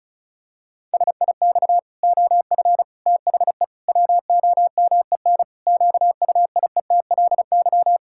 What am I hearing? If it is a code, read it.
Morse code, heard as a keyed tone, about 34 wpm, 700 Hz. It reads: SIX OF THE WOMEN QUIETLY
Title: Pangram Challenge v2 - 35wpm